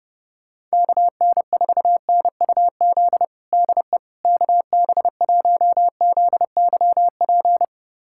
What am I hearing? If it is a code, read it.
Morse code, heard as a keyed tone, about 30 wpm, 700 Hz. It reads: KN4NUZ DE KB1ZYP
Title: USA Callsign DE Callsign 30wpm